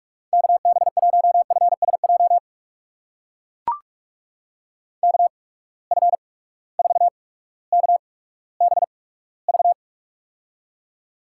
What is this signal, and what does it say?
Morse code, heard as a keyed tone, about 42 wpm, spaced as for 19 wpm, 700 Hz. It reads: KB1FSJ KF4KBV
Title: USA Call Signs - 6 Characters - 8x Character Spacing - 45wpm